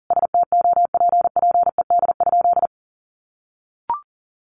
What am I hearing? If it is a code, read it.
Morse code, heard as a keyed tone, about 40 wpm, 700 Hz. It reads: STOPPED?
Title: A Sound Of Thunder 40wpm